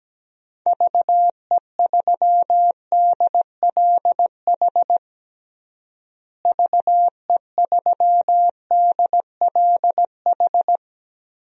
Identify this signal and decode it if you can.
Morse code, heard as a keyed tone, about 17 wpm, 700 Hz. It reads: VE3DLH VE3DLH
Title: DXCC-100 Call Signs - 6 Characters - Repeated 2x - 17wpm